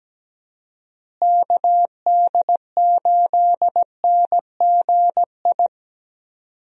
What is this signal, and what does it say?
Morse code, heard as a keyed tone, about 17 wpm, 700 Hz. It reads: KD8NGI
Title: USA Call Signs - 6 Characters - Repeated 2x - 17wpm